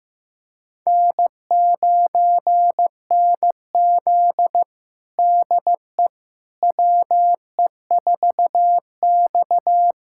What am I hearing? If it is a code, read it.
Morse code, heard as a keyed tone, about 15 wpm, 700 Hz. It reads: N9NZ DE WE4X
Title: USA Callsign DE Callsign - 4 Characters - 15wpm